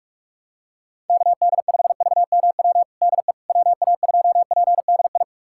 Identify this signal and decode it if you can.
Morse code, heard as a keyed tone, about 42 wpm, 700 Hz. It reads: KD5VMW DE WA2PDI
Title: USA Callsign DE Callsign - 6 Characters - 45wpm